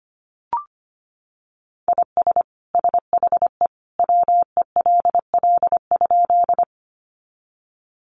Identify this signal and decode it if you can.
Morse code, heard as a keyed tone, about 25 wpm, 700 Hz. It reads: IS SHE WELL?
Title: Sentences from Top 200 Words 25wpm